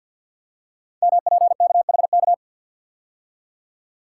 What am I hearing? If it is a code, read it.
Morse code, heard as a keyed tone, about 46 wpm, 700 Hz. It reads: MPKHK
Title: Code Groups - 5 Letter:Numbers - 50wpm